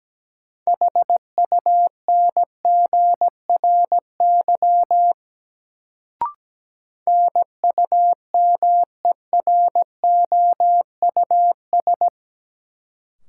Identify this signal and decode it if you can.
Morse code, heard as a keyed tone, about 17 wpm, 700 Hz. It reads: HUNGRY NUMEROUS
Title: Sentences from Top 1400 Words - New Word Drill - 17wpm